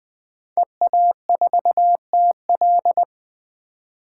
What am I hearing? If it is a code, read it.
Morse code, heard as a keyed tone, about 20 wpm, 700 Hz. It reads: EA4TL
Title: 2023 POTA Hunter Call Signs - 20wpm